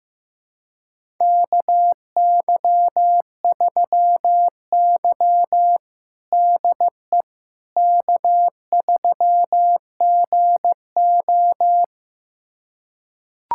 Morse code, heard as a keyed tone, about 15 wpm, 700 Hz. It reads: KY3Y DE K3GO
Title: USA Callsign DE Callsign - 4 Characters - 15wpm